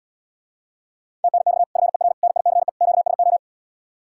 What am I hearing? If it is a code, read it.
Morse code, heard as a keyed tone, about 79 wpm, 700 Hz. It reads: TNX FER NICE QSO